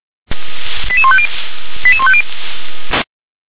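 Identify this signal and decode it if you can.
Five-tone selective call (ZVEI), heard as quick runs of five tones, tones of about 70 ms. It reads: 88150 88150
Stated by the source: ZVEI five-tone selective call 88158, received by SDR.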